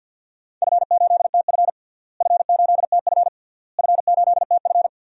Morse code, heard as a keyed tone, about 53 wpm, 700 Hz. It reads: F8TF F8TF F8TF
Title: DXCC-100 Call Signs - 4 Characters - Repeated 3x - 50wpm